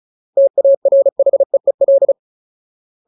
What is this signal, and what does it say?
Morse code, heard as a keyed tone, about 34 wpm, 550 Hz. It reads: TARHEEL
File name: Ultimate Word List / 35WPM Ultimate